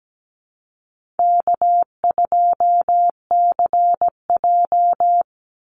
Morse code, heard as a keyed tone, about 17 wpm, 700 Hz. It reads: K2CJ
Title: Call Signs Repeated 3x - 17wpm